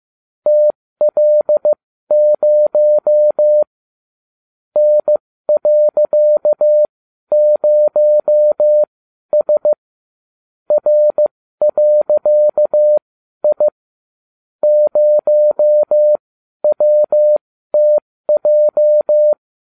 Morse code, heard as a keyed tone, about 15 wpm, spaced as for 12 wpm, 600 Hz. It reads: TL0 N.0S R.I 0WTJ